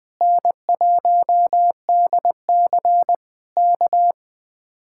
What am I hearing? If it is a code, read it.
Morse code, heard as a keyed tone, about 20 wpm, 700 Hz. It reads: N1DC K
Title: Speed-Racing Common QSO Phrases - 17wpm